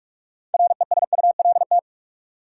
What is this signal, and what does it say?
Morse code, heard as a keyed tone, about 46 wpm, 700 Hz. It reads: RESULT
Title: Six Letter Words - 45wpm